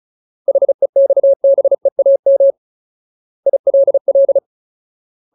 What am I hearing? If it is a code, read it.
Morse code, heard as a keyed tone, about 34 wpm, 550 Hz. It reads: HEXBEAM ILL
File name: Ultimate Word List / 35WPM Ultimate